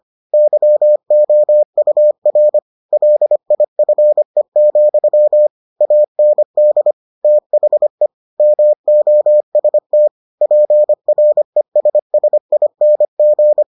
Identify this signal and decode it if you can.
Morse code, heard as a keyed tone, about 25 wpm, 600 Hz. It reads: YOUR LIFE, AND THE MOST PRESSING